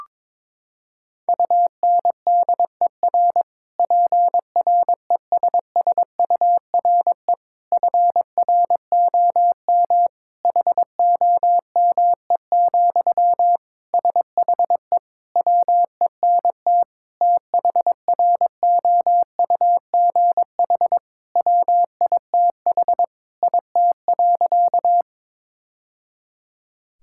Morse code, heard as a keyed tone, about 22 wpm, 700 Hz. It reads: UNDER PRESSURE FROM HOME, SHE WENT THROUGH WITH IT.
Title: Sentences from Top 1000 Words - Review Part 6 - 22wpm